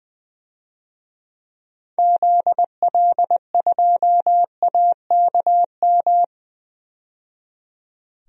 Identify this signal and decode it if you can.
Morse code, heard as a keyed tone, about 20 wpm, 700 Hz. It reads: ZL2AKM
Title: DXCC-100 Call Signs - 6 Characters - 20wpm